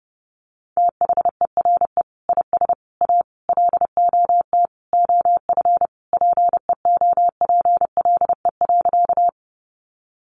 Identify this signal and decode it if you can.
Morse code, heard as a keyed tone, about 30 wpm, 700 Hz. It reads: THERE IS A LOT OF PEOPLE.
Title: Sentences from Top 500 Words 30wpm